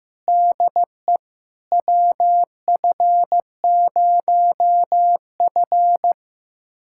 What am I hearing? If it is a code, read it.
Morse code, heard as a keyed tone, about 15 wpm, 700 Hz. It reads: DE WF0F
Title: USA Callsign DE Callsign - 4 Characters - 15wpm